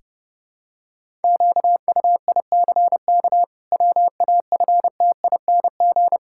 Morse code, heard as a keyed tone, about 30 wpm, 700 Hz. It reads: QUICK WAFTING
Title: Pangram Challenge v2 - 30wpm